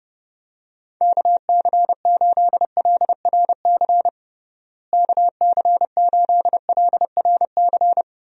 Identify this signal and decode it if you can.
Morse code, heard as a keyed tone, about 30 wpm, 700 Hz. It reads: KC8LRC KC8LRC
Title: USA Call Signs - 6 Characters - Repeated 3x - 30wpm